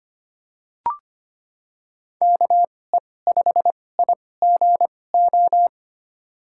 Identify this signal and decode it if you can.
Morse code, heard as a keyed tone, about 25 wpm, spaced as for 18 wpm, 700 Hz. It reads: KE5IGO
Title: USA Call Signs - 2x Character Spacing - 25wpm